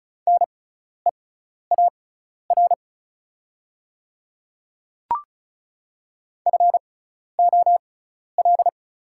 Morse code, heard as a keyed tone, about 36 wpm, spaced as for 12 wpm, 700 Hz. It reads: NEAR FOL
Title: 6x Character Spacing - Top 200 Words - 35wpm